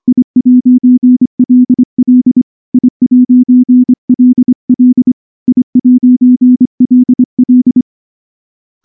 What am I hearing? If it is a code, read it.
Morse code, heard as a keyed tone, about 24 wpm, 250 Hz. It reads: I'LL I'LL I'LL
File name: morse / morse-training-25WPM